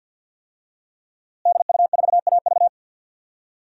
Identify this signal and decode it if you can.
Morse code, heard as a keyed tone, about 51 wpm, 700 Hz. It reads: DU4AV
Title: DXCC-100 Call Signs - 5 Characters - Repeated 2x - 50wpm